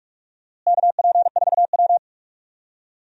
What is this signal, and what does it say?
Morse code, heard as a keyed tone, about 42 wpm, 700 Hz. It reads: KP4W
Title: DXCC-100 Call Signs - 4 Characters - Repeated 3x - 45wpm